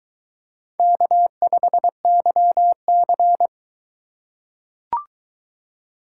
Morse code, heard as a keyed tone, about 22 wpm, 700 Hz. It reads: K5YC